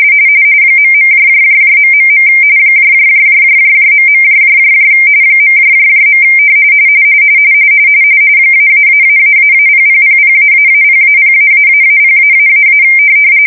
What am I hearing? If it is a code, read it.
RTTY, about 45.45 baud. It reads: ..... .. .. -------0000---0-0----0----